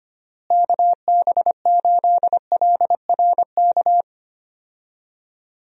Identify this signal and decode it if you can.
Morse code, heard as a keyed tone, about 25 wpm, 700 Hz. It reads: KB8LRK
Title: USA Call Signs - 6 Characters - Repeated 3x - 25wpm